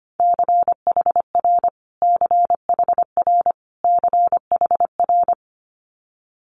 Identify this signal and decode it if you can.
Morse code, heard as a keyed tone, about 25 wpm, 700 Hz. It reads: CHR CHR CHR
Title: Multi-letter Phonemes Repeated 3x 25wpm